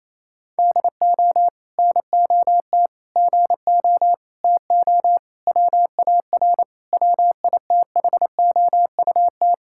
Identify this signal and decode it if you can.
Morse code, heard as a keyed tone, about 28 wpm, 700 Hz. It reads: DO NOT GO TO WAR WITHOUT
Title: Sentences from Top 1000 Words - Review Part 5 - 28wpm